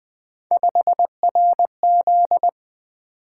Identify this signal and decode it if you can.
Morse code, heard as a keyed tone, about 20 wpm, 700 Hz. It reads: 5RZ
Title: Three Letter-Number Combinations - 20wpm